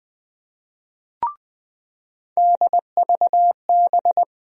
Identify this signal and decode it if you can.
Morse code, heard as a keyed tone, about 20 wpm, 700 Hz. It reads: DVB
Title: USA Call Sign Postfixes - 3 Letters 20wpm